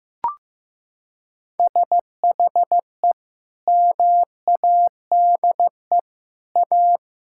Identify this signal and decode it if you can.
Morse code, heard as a keyed tone, about 15 wpm, 700 Hz. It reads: SHE MADE A
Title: Sentences from Top 1000 Words - Review Part 6 - 15wpm